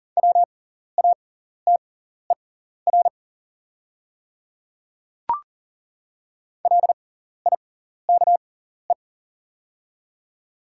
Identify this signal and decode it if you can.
Morse code, heard as a keyed tone, about 40 wpm, spaced as for 14 wpm, 700 Hz. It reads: WATER LIKE
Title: 6x Character Spacing - Top 100 Words - 40wpm